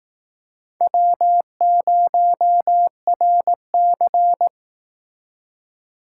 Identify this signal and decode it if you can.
Morse code, heard as a keyed tone, about 18 wpm, 700 Hz. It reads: W0RC
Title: Speed Racing - USA 1x2 Call Signs 15wpm